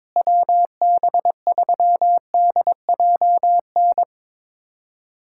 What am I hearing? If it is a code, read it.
Morse code, heard as a keyed tone, about 22 wpm, 700 Hz. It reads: WB3DJN
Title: USA Callsign DE Callsign 22wpm